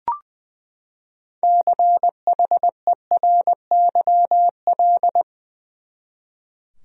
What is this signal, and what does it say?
Morse code, heard as a keyed tone, about 21 wpm, 700 Hz. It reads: CHERYL
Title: USA Common Names - 20wpm